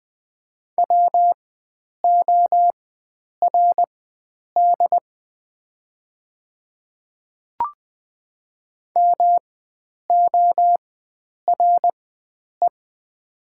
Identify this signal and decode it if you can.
Morse code, heard as a keyed tone, about 20 wpm, spaced as for 9 wpm, 700 Hz. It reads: WORD MORE
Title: 4x Character Spacing - Top 100 Words - 20wpm